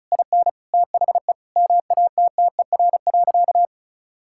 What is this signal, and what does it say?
Morse code, heard as a keyed tone, about 36 wpm, 700 Hz. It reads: IN THE MATTER.
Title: Sentences from Top 1000 Words - Review Part 6 - 35wpm